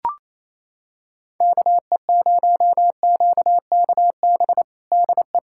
Morse code, heard as a keyed tone, about 27 wpm, 700 Hz. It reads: KE0QKB DE
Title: USA Callsign DE Callsign - 6 Characters - 28wpm